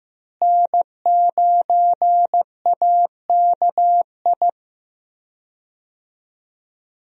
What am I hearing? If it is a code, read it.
Morse code, heard as a keyed tone, about 15 wpm, 700 Hz. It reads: N9AKI
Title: USA Call Signs - 5 Characters - Repeated 3x - 15wpm